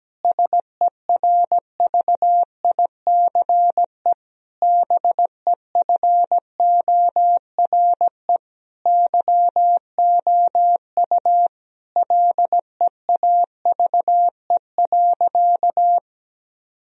Morse code, heard as a keyed tone, about 17 wpm, 700 Hz. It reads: SERVICE BEFORE YOU LEAVE.
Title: Sentences from Top 1000 Words - Review Part 4 - 17wpm